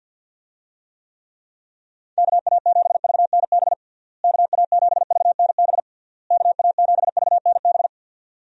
Morse code, heard as a keyed tone, about 46 wpm, 700 Hz. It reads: KA7VNB KA7VNB KA7VNB
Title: USA Call Signs - 6 Characters - Repeated 3x - 50wpm